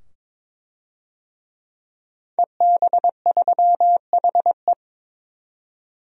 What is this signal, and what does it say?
Morse code, heard as a keyed tone, about 21 wpm, 700 Hz. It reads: EB3HE
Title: DXCC-100 Call Signs - 5 Characters - 22wpm